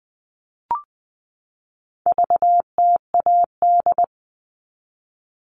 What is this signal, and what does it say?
Morse code, heard as a keyed tone, about 20 wpm, 700 Hz. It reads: VTAD
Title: All Cut Numbers - 4-Digit 20wpm